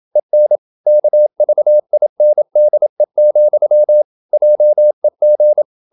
Morse code, heard as a keyed tone, about 27 wpm, 600 Hz. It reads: EN KVINDE, JEG
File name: Ferie QRQ 27wpm